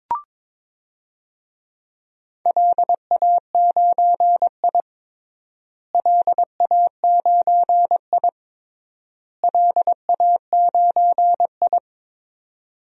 Morse code, heard as a keyed tone, about 22 wpm, 700 Hz. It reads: LA9I LA9I LA9I
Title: DXCC-100 Call Signs - 4 Characters - Repeated 3x - 22wpm